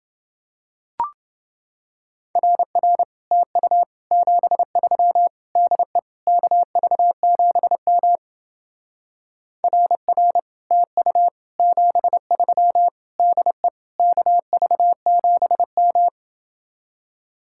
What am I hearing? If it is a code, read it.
Morse code, heard as a keyed tone, about 28 wpm, 700 Hz. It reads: RR TU 73 DE KV7M RR TU 73 DE KV7M